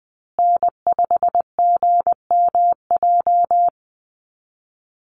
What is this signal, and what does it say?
Morse code, heard as a keyed tone, about 20 wpm, 700 Hz. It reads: N5GMJ
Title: USA Call Signs Repeated 2x 20wpm